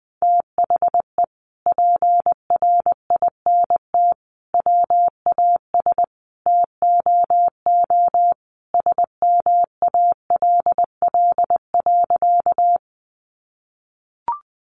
Morse code, heard as a keyed tone, about 19 wpm, 700 Hz. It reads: THE PRINT WAS TOO SMALL.
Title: Sentences from Top 1400 Words 20wpm